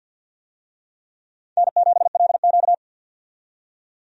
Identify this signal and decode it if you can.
Morse code, heard as a keyed tone, about 51 wpm, 700 Hz. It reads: N7LQ
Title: USA Call Signs - 4 Characters - 50wpm